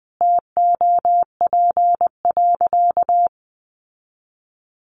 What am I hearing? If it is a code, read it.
Morse code, heard as a keyed tone, about 20 wpm, 700 Hz. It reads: TOP.
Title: Sentences from Top 1300 Words 20wpm